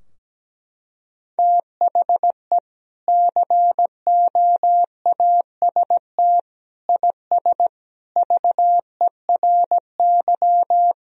Morse code, heard as a keyed tone, about 17 wpm, 700 Hz. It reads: THE COAST IS VERY
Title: Sentences from Top 1000 Words - Review Part 4 - 17wpm